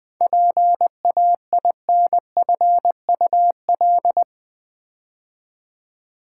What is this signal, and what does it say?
Morse code, heard as a keyed tone, about 20 wpm, 700 Hz. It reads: PAINFUL
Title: 7 Letter Words - First 3 Letters Given - 20wpm